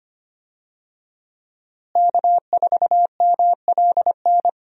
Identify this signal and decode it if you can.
Morse code, heard as a keyed tone, about 25 wpm, 700 Hz. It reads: K4MLN
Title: USA Call Signs - 5 Characters - Repeated 3x - 25wpm